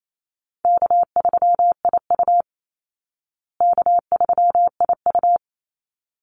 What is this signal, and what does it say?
Morse code, heard as a keyed tone, about 28 wpm, 700 Hz. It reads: K3IU K3IU
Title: Easy Call Signs Repeated 2x - 28wpm